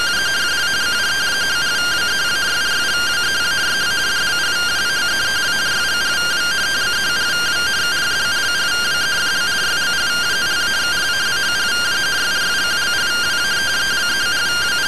Two-tone teleprinter signal, about 170 Hz shift.